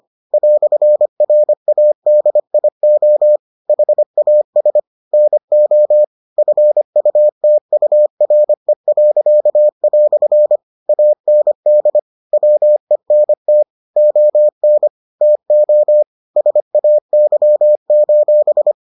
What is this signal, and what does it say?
Morse code, heard as a keyed tone, about 25 wpm, 600 Hz. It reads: "RADIO HAS NO FUTURE." AND WENT ON TO SAY: